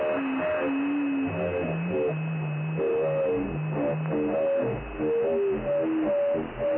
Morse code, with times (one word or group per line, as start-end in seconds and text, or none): none